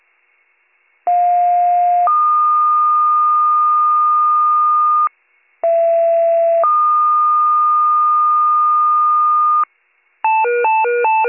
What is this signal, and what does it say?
Two-tone page, heard as a one-second tone, then a three-second tone, about 710 then 1220 Hz, about 670 then 1220 Hz. It is heard from 1.1 to 9.6 s.